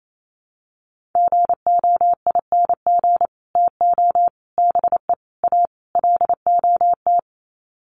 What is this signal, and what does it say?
Morse code, heard as a keyed tone, about 28 wpm, 700 Hz. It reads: GOING TO BE A LOT
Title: Sets of 5 Words Encore 28wpm